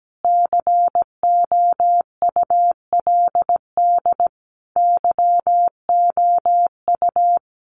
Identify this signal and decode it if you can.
Morse code, heard as a keyed tone, about 17 wpm, 700 Hz. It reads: COULD YOU
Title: Sentences from Top 1400 Words 17wpm